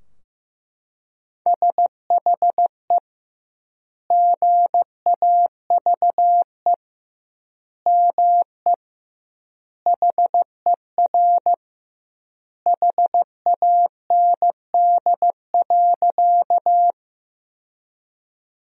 Morse code, heard as a keyed tone, about 15 wpm, 700 Hz. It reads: SHE GAVE ME HER HAND.
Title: 2x Word-Spacing - Sentences from Top 500 Words - 15wpm